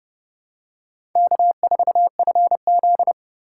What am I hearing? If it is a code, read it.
Morse code, heard as a keyed tone, about 27 wpm, 700 Hz. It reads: K4FZ